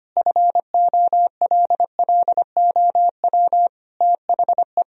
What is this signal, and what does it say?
Morse code, heard as a keyed tone, about 25 wpm, 700 Hz. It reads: FOLLOW THE
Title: Sentences from Top 1000 Words - Review Part 4 - 25wpm